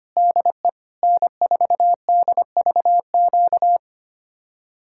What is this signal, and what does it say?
Morse code, heard as a keyed tone, about 30 wpm, 700 Hz. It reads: DE N4DVQ
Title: Speed-Racing Common QSO Phrases - 25wpm